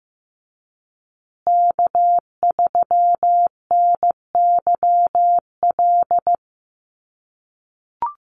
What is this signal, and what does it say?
Morse code, heard as a keyed tone, about 15 wpm, 700 Hz. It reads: K3NYL
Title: USA Call Signs Repeated 2x 15wpm